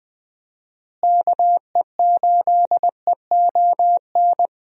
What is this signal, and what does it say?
Morse code, heard as a keyed tone, about 21 wpm, 700 Hz. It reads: KE8EON
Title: USA Call Signs - 6 Characters - Repeated 2x - 20wpm